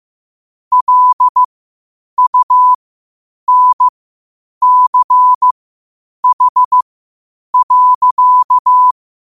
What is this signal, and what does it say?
Morse code, heard as a keyed tone, about 15 wpm, spaced as for 8 wpm, 1000 Hz. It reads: LUNCH.
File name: qotdfiles / 8wpm